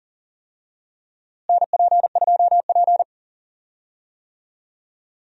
Morse code, heard as a keyed tone, about 37 wpm, 700 Hz. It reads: NP2P
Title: USA Call Signs - 4 Characters - Repeated 3x - 40wpm